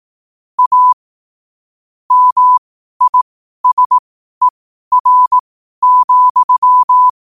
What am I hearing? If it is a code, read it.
Morse code, heard as a keyed tone, about 18 wpm, spaced as for 12 wpm, 1000 Hz. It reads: A MISER,
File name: qotdfiles / 12wpm